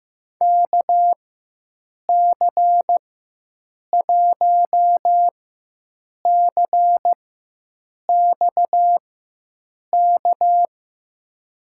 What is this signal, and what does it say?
Morse code, heard as a keyed tone, about 15 wpm, spaced as for 7 wpm, 700 Hz. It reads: KC1CXK
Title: USA Call Signs - 4x Character Spacing - 15wpm